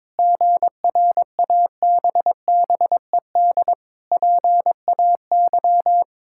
Morse code, heard as a keyed tone, about 22 wpm, 700 Hz. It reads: GRABBED PAY
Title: Pangram Challenge v2 - 22wpm